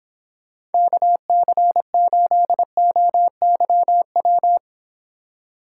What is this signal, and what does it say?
Morse code, heard as a keyed tone, about 30 wpm, 700 Hz. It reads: KC8OYW